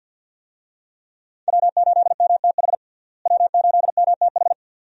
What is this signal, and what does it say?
Morse code, heard as a keyed tone, about 50 wpm, 700 Hz. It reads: W8MTH W8MTH
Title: USA Call Signs - 5 Characters - Repeated 2x - 50wpm